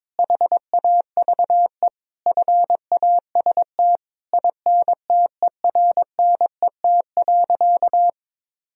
Morse code, heard as a keyed tone, about 21 wpm, 700 Hz. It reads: HAVE FAST INTERNET.
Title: Sentences from Top 1000 Words - Review Part 4 - 22wpm